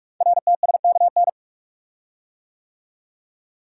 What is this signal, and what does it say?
Morse code, heard as a keyed tone, about 44 wpm, 700 Hz. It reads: RTSKN
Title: Code Groups - 5 Letters - 45wpm